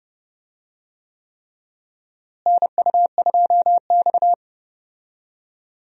Morse code, heard as a keyed tone, about 31 wpm, 700 Hz. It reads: NU2X
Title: USA Call Signs - 4 Characters - Repeated 3x - 30wpm